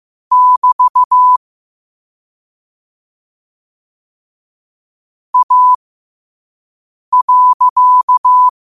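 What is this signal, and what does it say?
Morse code, heard as a keyed tone, about 15 wpm, spaced as for 5 wpm, 1000 Hz. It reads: <BT> A.
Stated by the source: Morse code sent at 5 wpm.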